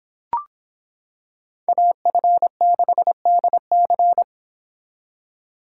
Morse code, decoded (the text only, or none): AF6DC